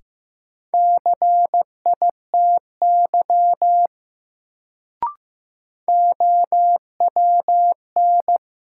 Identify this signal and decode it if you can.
Morse code, heard as a keyed tone, about 15 wpm, 700 Hz. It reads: CITY OWN
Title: Sets of 2 Words - New Word Drill - 15wpm